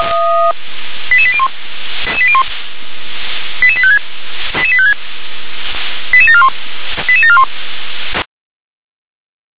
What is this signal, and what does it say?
Five-tone selective call (ZVEI), heard as quick runs of five tones, tones of about 70 ms. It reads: 88801 88856 88741 88741